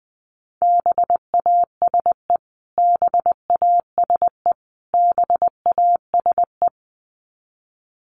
Morse code, heard as a keyed tone, about 20 wpm, 700 Hz. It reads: BASE BASE BASE
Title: Top 500 Words Repeated 20wpm